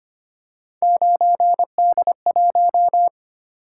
Morse code, heard as a keyed tone, about 25 wpm, 700 Hz. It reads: 9D1
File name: Three Letter-Number Combinations - 25wpm